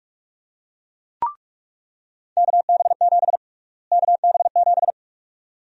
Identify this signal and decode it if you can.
Morse code, heard as a keyed tone, about 43 wpm, 700 Hz. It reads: KB7 KB7
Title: USA Call Sign Prefixes - Repeated 2x - 45wpm